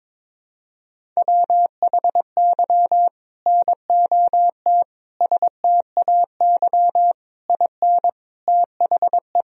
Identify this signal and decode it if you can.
Morse code, heard as a keyed tone, about 22 wpm, 700 Hz. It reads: WHY NOT STAY IN THE
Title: Sentences from Top 1000 Words - Review Part 1 - 22wpm